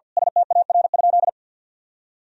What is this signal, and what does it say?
Morse code, heard as a keyed tone, about 52 wpm, 700 Hz. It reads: STAR?
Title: Sentences from Top 1000 Words - Review Part 2 - 50wpm